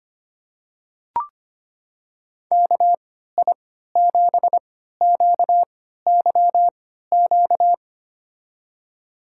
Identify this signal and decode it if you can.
Morse code, heard as a keyed tone, about 25 wpm, spaced as for 14 wpm, 700 Hz. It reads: KI7QYQ
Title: USA Call Signs - 3x Character Spacing - 25wpm